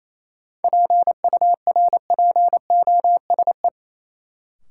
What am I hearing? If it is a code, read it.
Morse code, heard as a keyed tone, about 28 wpm, 700 Hz. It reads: PURPOSE
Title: Sentences from Top 800 Words - New Word Drill - 28wpm